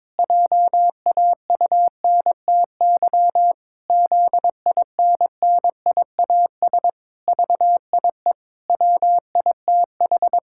Morse code, heard as a keyed tone, about 22 wpm, 700 Hz. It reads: JAUNTY ZINNIAS VIE WITH
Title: Pangram Challenge v2 - 22wpm